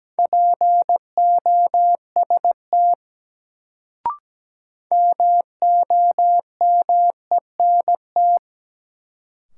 Morse code, heard as a keyed tone, about 17 wpm, 700 Hz. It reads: POST MOMENT
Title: Sentences from Top 1000 Words - New Word Drill - 17wpm